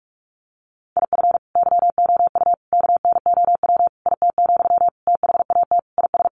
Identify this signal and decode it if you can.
Morse code, heard as a keyed tone, about 46 wpm, 700 Hz. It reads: IF YOU KNOW IT, THAT IS
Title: Sentences from Top 400 Words 45wpm